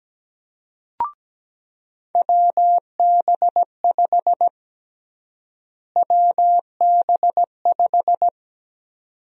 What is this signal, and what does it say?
Morse code, heard as a keyed tone, about 17 wpm, 700 Hz. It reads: WB5 WB5
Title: USA Call Sign Prefixes - Repeated 2x - 17wpm